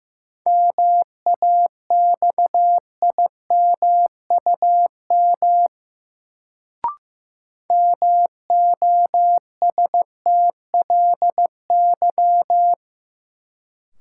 Morse code, heard as a keyed tone, about 15 wpm, 700 Hz. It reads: MAXIMUM MOSTLY
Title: Sentences from Top 1300 Words - New Word Drill - 15wpm